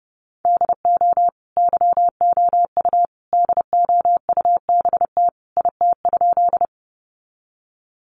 Morse code, heard as a keyed tone, about 30 wpm, 700 Hz. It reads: DO YOU DOUBT IT?
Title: Sentences from Top 1500 Words 30wpm